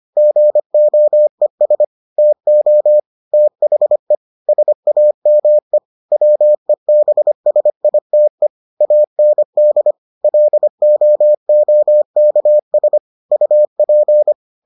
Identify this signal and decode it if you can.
Morse code, heard as a keyed tone, about 25 wpm, 600 Hz. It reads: GOES TO THE SAME WEBSITE AND LOOKS UP